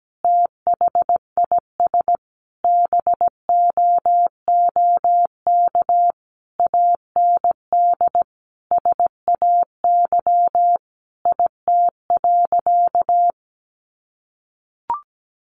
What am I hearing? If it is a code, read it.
Morse code, heard as a keyed tone, about 17 wpm, 700 Hz. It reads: THIS BOOK AND SAY IT.
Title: Sentences from Top 400 Words 17wpm